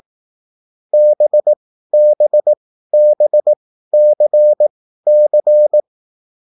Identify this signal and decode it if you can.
Morse code, heard as a keyed tone, about 18 wpm, spaced as for 13 wpm, 600 Hz. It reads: BBBCC
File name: QSL / CW lesson